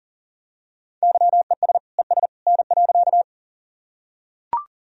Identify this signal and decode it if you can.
Morse code, heard as a keyed tone, about 40 wpm, 700 Hz. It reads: YES ES N.
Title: Sentences from Top 500 Words and Abbreviations 40wpm